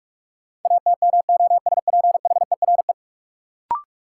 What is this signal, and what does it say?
Morse code, heard as a keyed tone, about 43 wpm, 700 Hz. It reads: ATMOSPHERE
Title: Sentences from Top 1600 Words - New Word Drill - 45wpm